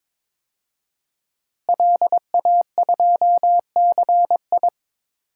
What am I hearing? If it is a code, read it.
Morse code, heard as a keyed tone, about 22 wpm, 700 Hz. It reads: LA2CI
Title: DXCC-100 Call Signs - 5 Characters - Repeated 2x - 22wpm